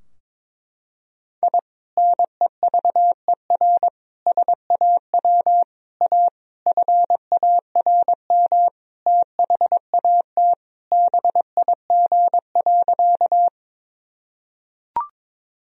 Morse code, heard as a keyed tone, about 22 wpm, 700 Hz. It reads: I NEVER SAW A FARM THAT BIG.
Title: Sentences from Top 1000 Words - Review Part 2 - 22wpm